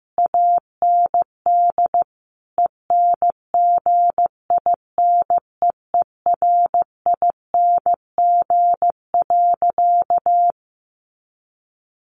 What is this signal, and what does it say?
Morse code, heard as a keyed tone, about 15 wpm, 700 Hz. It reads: AND ENGINEERING.
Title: Sentences from Top 1500 Words 15wpm